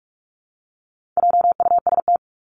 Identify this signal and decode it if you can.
Morse code, heard as a keyed tone, about 44 wpm, 700 Hz. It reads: JUST
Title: Top 300 Words Repeated 45wpm